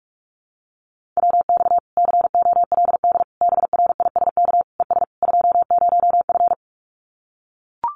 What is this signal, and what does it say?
Morse code, heard as a keyed tone, about 44 wpm, 700 Hz. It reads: WX COLD BRISK ES 20F
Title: Common QSO Phrases Encore 45wpm